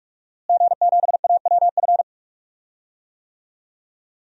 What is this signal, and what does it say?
Morse code, heard as a keyed tone, about 41 wpm, 700 Hz. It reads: G7AWF
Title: DXCC-100 Call Signs - 5 Characters - Repeated 2x - 45wpm